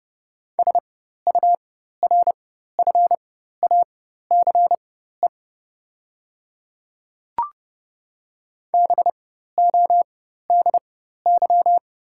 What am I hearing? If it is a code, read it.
Morse code, heard as a keyed tone, about 30 wpm, spaced as for 14 wpm, 690 Hz. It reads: SURFACE BODY
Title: 4x Character Spacing - Top 500 Words - 30wpm